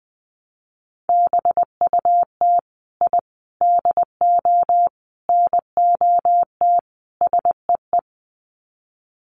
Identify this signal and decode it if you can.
Morse code, heard as a keyed tone, about 20 wpm, 700 Hz. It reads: BUT I DO NOT SEE
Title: Sets of 5 Words Encore 20wpm